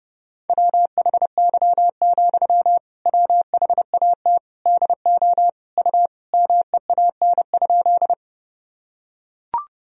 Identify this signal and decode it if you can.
Morse code, heard as a keyed tone, about 32 wpm, 700 Hz. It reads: WHY, WHAT DO U MEAN?
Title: Sentences from Top 200 Words and Abbreviations 30wpm